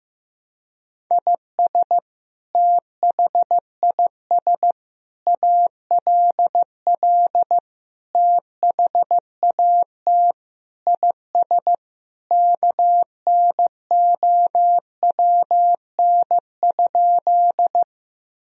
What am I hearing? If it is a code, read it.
Morse code, heard as a keyed tone, about 15 wpm, 700 Hz. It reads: IS THIS ALL THAT IS KNOWN?
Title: Sentences from Top 1000 Words - Review Part 3 - 15wpm